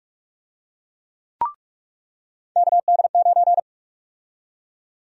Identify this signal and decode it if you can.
Morse code, heard as a keyed tone, about 48 wpm, 700 Hz. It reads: KD9
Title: USA Call Sign Prefixes - 45wpm